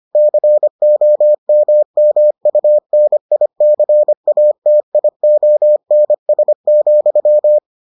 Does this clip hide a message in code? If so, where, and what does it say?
Yes: Morse code from 0.2 to 7.6 s, heard as a keyed tone, about 25 wpm, 600 Hz. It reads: COMMUNICATIONS,